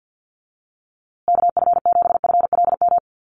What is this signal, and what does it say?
Morse code, heard as a keyed tone, about 48 wpm, 700 Hz. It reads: KF7FLM
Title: USA Call Signs Repeated 3x 50wpm